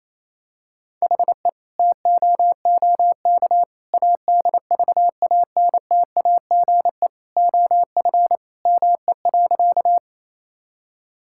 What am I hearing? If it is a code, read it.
Morse code, heard as a keyed tone, about 28 wpm, 700 Hz. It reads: HE TOOK ADVANTAGE OF ME.
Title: Sentences from Top 1000 Words - Review Part 6 - 28wpm